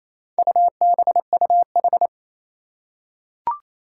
Morse code, heard as a keyed tone, about 28 wpm, 700 Hz. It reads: UBUH
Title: Four Challenging Letter-Number Combinations 28wpm